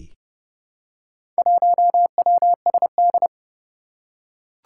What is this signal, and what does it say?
Morse code, heard as a keyed tone, about 29 wpm, 700 Hz. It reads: 1WSD
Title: Four Challenging Letter-Number Combinations 30wpm